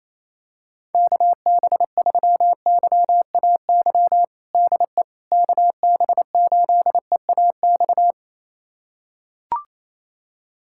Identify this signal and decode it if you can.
Morse code, heard as a keyed tone, about 28 wpm, 700 Hz. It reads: KB3YAY DE KB8EAX
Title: USA Callsign DE Callsign - 6 Characters - 28wpm